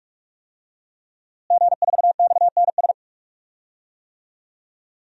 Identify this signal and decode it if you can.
Morse code, heard as a keyed tone, about 42 wpm, 700 Hz. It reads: G4XNS
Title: DXCC-100 Call Signs - 5 Characters - Repeated 2x - 45wpm